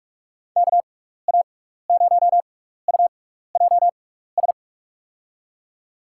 Morse code, heard as a keyed tone, about 46 wpm, spaced as for 15 wpm, 700 Hz. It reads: KA0UJS
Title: USA Call Signs - 6 Characters - 6x Character Spacing - 45wpm